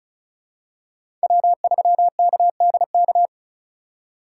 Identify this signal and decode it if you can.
Morse code, heard as a keyed tone, about 36 wpm, 700 Hz. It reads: W3KDK
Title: USA Call Signs - 5 Characters - 4x Character Spacing - 35wpm